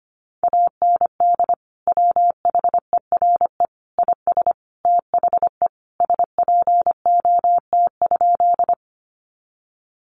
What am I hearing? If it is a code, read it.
Morse code, heard as a keyed tone, about 25 wpm, 700 Hz. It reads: AND WHERE IS THE SPOT?
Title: Sentences from Top 1100 Words 25wpm